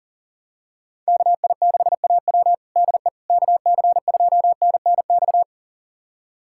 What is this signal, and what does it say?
Morse code, heard as a keyed tone, about 40 wpm, 700 Hz. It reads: KI6AW DE KC2NNX
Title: USA Callsign DE Callsign 40wpm